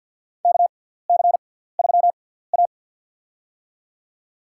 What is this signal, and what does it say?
Morse code, heard as a keyed tone, about 48 wpm, spaced as for 17 wpm, 700 Hz. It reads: KC3A
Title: USA Call Signs - 4 Characters - 6x Character Spacing - 50wpm